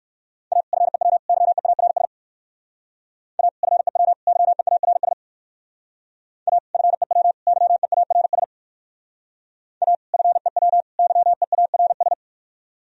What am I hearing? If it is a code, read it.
Morse code, heard as a keyed tone, about 61 wpm, 700 Hz. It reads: A FEW YEARS A FEW YEARS A FEW YEARS A FEW YEARS